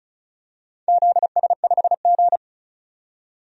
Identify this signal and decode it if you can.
Morse code, heard as a keyed tone, about 36 wpm, 700 Hz. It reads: ZS5G
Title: DXCC-100 Call Signs - 4 Characters - 35wpm